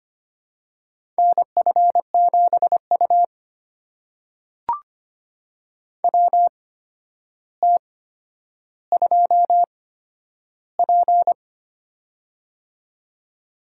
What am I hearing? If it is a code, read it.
Morse code, heard as a keyed tone, about 25 wpm, spaced as for 11 wpm, 700 Hz. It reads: NF7U WT2P
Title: USA Call Signs - 4 Characters - 8x Character Spacing - 25wpm